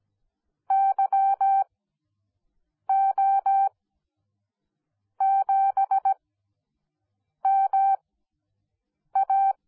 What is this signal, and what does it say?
Morse code, heard as a keyed tone, about 17 wpm, spaced as for 6 wpm, 780 Hz. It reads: YO7MA